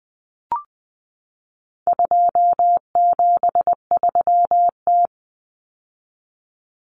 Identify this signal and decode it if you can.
Morse code, heard as a keyed tone, about 20 wpm, 700 Hz. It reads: 273T
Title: Common Cut Numbers - 4-Digit 20wpm